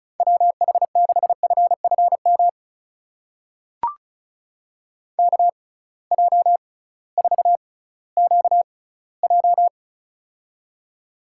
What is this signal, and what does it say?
Morse code, heard as a keyed tone, about 36 wpm, spaced as for 19 wpm, 700 Hz. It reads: WH6FFM KJ4QJ
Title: USA Call Signs - 6x Character Spacing - 35wpm